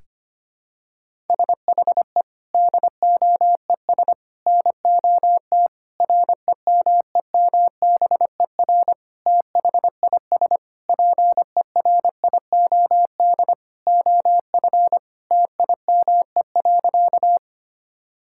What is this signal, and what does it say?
Morse code, heard as a keyed tone, about 25 wpm, 700 Hz. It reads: SHE DOES NOT REMEMBER THIS PERIOD OF TIME.
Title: Sentences from Top 1000 Words - Review Part 5 - 25wpm